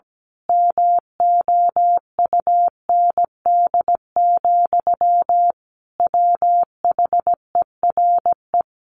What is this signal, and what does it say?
Morse code, heard as a keyed tone, about 17 wpm, 700 Hz. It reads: MOUND, WHERE
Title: A Sound Of Thunder 17wpm